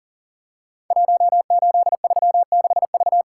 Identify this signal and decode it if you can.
Morse code, heard as a keyed tone, about 42 wpm, 700 Hz. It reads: 1836V
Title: Code Groups - 5 Letter:Numbers - 40wpm